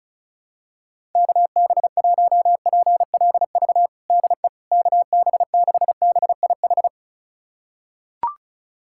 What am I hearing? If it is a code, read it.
Morse code, heard as a keyed tone, about 34 wpm, 700 Hz. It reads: KB1PLV DE KB6BIH
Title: USA Callsign DE Callsign - 6 Characters - 35wpm